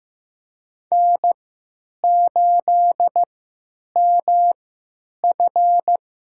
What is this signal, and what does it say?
Morse code, heard as a keyed tone, about 15 wpm, spaced as for 9 wpm, 700 Hz. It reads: N8MF
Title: USA Call Signs - 3x Character Spacing - 15wpm